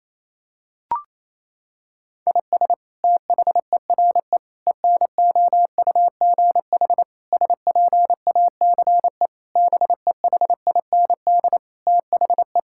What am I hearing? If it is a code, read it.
Morse code, heard as a keyed tone, about 28 wpm, 700 Hz. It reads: IS THERE ENOUGH SPACE BEHIND THE
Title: Sentences from Top 1000 Words - Review Part 5 - 28wpm